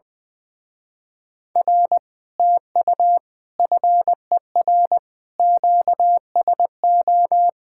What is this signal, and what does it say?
Morse code, heard as a keyed tone, about 20 wpm, 700 Hz. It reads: R TU FER QSO
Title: Speed-Racing Common QSO Phrases - 20wpm